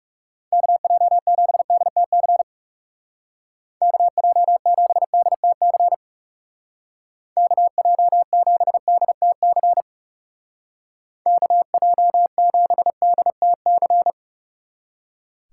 Morse code, heard as a keyed tone, about 35 wpm, 700 Hz. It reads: KJ7DTC KJ7DTC KJ7DTC KJ7DTC
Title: Speed Racing - USA 2x3 Call Signs 30wpm